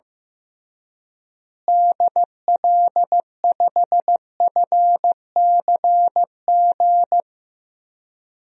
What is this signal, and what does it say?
Morse code, heard as a keyed tone, about 15 wpm, 700 Hz. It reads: DL5FCG
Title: DXCC-100 Call Signs - 6 Characters - Repeated 2x - 15wpm